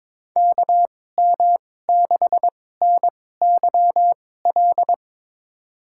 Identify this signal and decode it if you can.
Morse code, heard as a keyed tone, about 21 wpm, spaced as for 16 wpm, 700 Hz. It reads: KM6NYL
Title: USA Call Signs - 6 Characters - 2x Character Spacing - 22wpm